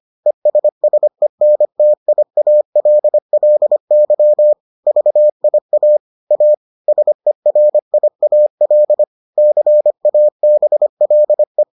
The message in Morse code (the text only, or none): ESSENTIALLY VIA A SERIAL CABLE